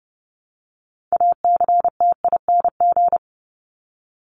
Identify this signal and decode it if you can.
Morse code, heard as a keyed tone, about 30 wpm, 700 Hz. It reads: ACTING
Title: ING Suffix - Top 100 Words with Base Words 30wpm